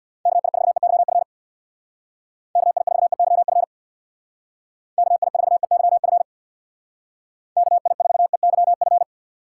Morse code, heard as a keyed tone, about 66 wpm, 700 Hz. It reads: KI4EYF KI4EYF KI4EYF KI4EYF